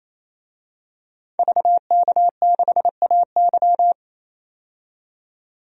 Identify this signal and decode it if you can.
Morse code, heard as a keyed tone, about 28 wpm, 700 Hz. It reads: VK6AY
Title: DXCC-100 Call Signs - 5 Characters - 28wpm